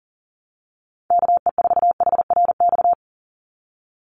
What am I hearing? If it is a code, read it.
Morse code, heard as a keyed tone, about 40 wpm, 700 Hz. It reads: KE4HRX
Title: USA Call Signs 40wpm